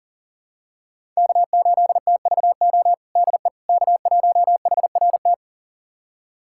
Morse code, heard as a keyed tone, about 40 wpm, 700 Hz. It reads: K8TVO DE K1HRT
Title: USA Callsign DE Callsign - 5 Characters - 40wpm